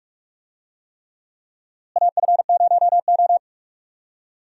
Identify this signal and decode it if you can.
Morse code, heard as a keyed tone, about 45 wpm, 700 Hz. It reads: AF0O
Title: USA Call Signs - 4 Characters - Repeated 3x - 45wpm